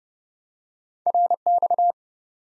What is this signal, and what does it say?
Morse code, heard as a keyed tone, about 30 wpm, 700 Hz. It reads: RX
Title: Top 90 Abbr 30WPM